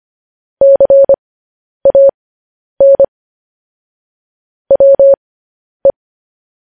Morse code, heard as a keyed tone, about 25 wpm, spaced as for 10 wpm, 550 Hz. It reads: CAN WE
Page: News Headlines in Morse Code at 10 WPM